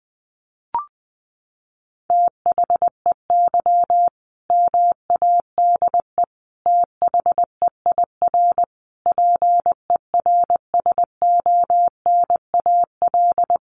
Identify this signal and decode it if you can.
Morse code, heard as a keyed tone, about 20 wpm, 700 Hz. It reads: THEY MADE THEIR PERSONAL
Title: Sentences from Top 1100 Words 20wpm